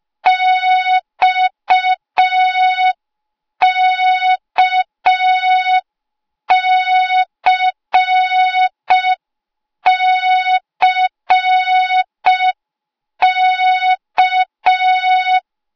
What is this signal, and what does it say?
Morse code, heard as a keyed tone, about 5 wpm, 3700 Hz. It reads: XKCCK
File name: Morze-HA2MN / hang / 08KXC HA2MN